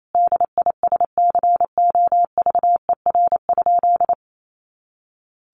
Morse code, heard as a keyed tone, about 28 wpm, 700 Hz. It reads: DISCOVER?
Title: Sentences from Top 1100 Words 28wpm